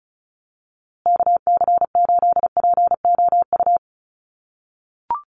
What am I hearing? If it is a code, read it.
Morse code, heard as a keyed tone, about 34 wpm, 700 Hz. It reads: KC8POU
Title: USA Call Signs Repeated 3x 35wpm